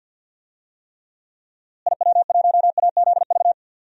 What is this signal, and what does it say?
Morse code, heard as a keyed tone, about 47 wpm, 700 Hz. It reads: IW1AZV
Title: DXCC-100 Call Signs - 6 Characters - Repeated 3x - 50wpm